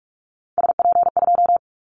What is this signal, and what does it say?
Morse code, heard as a keyed tone, about 44 wpm, 700 Hz. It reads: SP2
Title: Callsign Prefixes for Top 25 DXCC - 45wpm